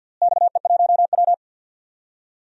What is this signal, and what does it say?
Morse code, heard as a keyed tone, about 50 wpm, 700 Hz. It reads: XE1W
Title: DXCC-100 Call Signs - 4 Characters - Repeated 2x - 50wpm